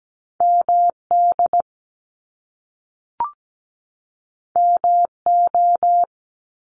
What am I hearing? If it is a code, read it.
Morse code, heard as a keyed tone, about 17 wpm, 700 Hz. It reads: MD MO
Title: US States 17wpm